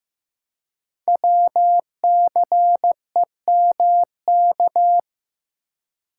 Morse code, heard as a keyed tone, about 15 wpm, 700 Hz. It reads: WCEMK
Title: Code Groups - 5 Letters - 15wpm